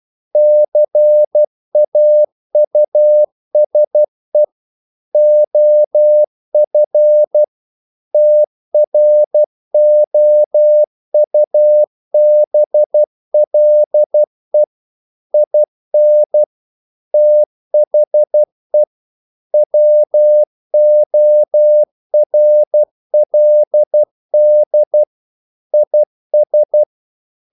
Morse code, heard as a keyed tone, about 12 wpm, 600 Hz. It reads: CAUSE OF TROUBLE IN THE WORLD IS
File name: CW / Practice / EnglishQuotations0112